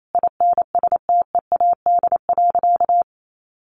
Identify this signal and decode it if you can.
Morse code, heard as a keyed tone, about 28 wpm, 700 Hz. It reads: INSTEAD.
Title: Sentences from Top 800 Words 28wpm